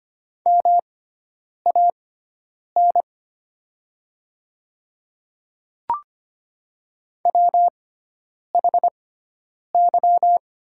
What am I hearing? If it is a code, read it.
Morse code, heard as a keyed tone, about 25 wpm, spaced as for 9 wpm, 700 Hz. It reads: MAN WHY
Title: 6x Character Spacing - Top 200 Words - 25wpm